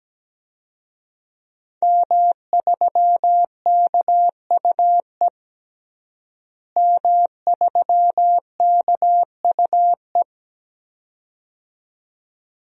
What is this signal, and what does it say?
Morse code, heard as a keyed tone, about 17 wpm, 700 Hz. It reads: M3KUE M3KUE
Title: DXCC-100 Call Signs - 5 Characters - Repeated 2x - 17wpm